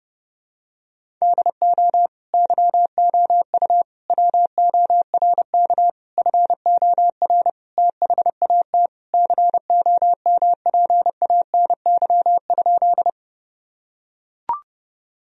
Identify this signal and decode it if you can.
Morse code, heard as a keyed tone, about 30 wpm, 700 Hz. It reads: DO YOU WORK FOR THAT COMPANY?
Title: Sentences from Top 1000 Words - Review Part 3 - 30wpm